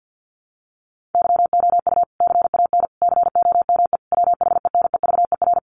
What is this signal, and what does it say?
Morse code, heard as a keyed tone, about 50 wpm, 700 Hz. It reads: YOU CAN COME WHENEVER
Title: Sentences from Top 1200 Words 50wpm